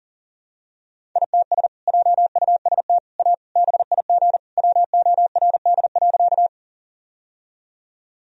Morse code, heard as a keyed tone, about 40 wpm, 700 Hz. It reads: ITS JUST A BIG WORD.
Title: Sentences from Top 1000 Words - Review Part 1 - 40wpm